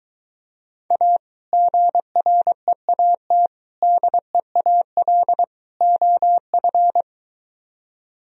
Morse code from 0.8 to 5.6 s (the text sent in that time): A GREAT DEAL